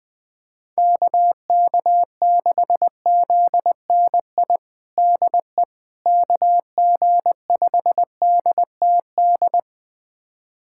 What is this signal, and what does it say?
Morse code, heard as a keyed tone, about 20 wpm, 700 Hz. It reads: KK6ZNI DE KG5DTD
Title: USA Callsign DE Callsign - 6 Characters - 20wpm